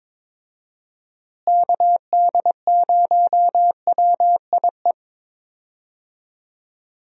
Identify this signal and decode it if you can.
Morse code, heard as a keyed tone, about 22 wpm, 700 Hz. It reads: KD0WIE
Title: USA Call Signs - 6 Characters - Repeated 2x - 22wpm